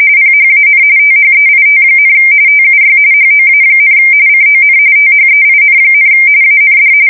RTTY, about 45.45 baud. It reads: PHAN THEIT, TUY HOA, CON THEIN, AND
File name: fsk170 / USS-New-Jersey